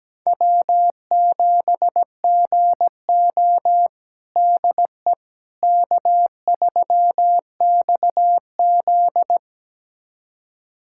Morse code, heard as a keyed tone, about 17 wpm, 700 Hz. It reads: W7GO DE K3XZ
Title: USA Callsign DE Callsign - 4 Characters - 17wpm